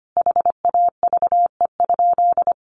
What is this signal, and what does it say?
Morse code, heard as a keyed tone, about 25 wpm, 700 Hz. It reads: HAVE?
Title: Sentences from Top 1000 Words 25wpm